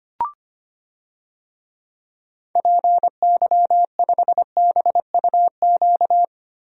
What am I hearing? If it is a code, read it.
Morse code, heard as a keyed tone, about 25 wpm, 700 Hz. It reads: PY5BUQ
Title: DXCC-100 Call Signs - 6 Characters - Repeated 3x - 25wpm